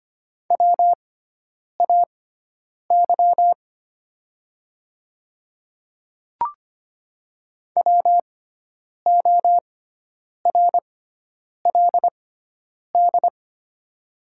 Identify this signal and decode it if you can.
Morse code, heard as a keyed tone, about 25 wpm, spaced as for 9 wpm, 700 Hz. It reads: WAY WORLD
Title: 6x Character Spacing - Top 200 Words - 25wpm